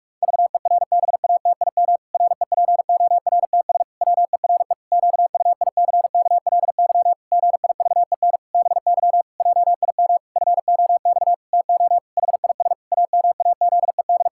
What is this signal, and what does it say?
Morse code, heard as a keyed tone, about 46 wpm, 700 Hz. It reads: VERBATIM REPORTS WERE QUICKLY GIVEN BY JIM FOX TO HIS AMAZED
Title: Pangram Challenge v2 - 45wpm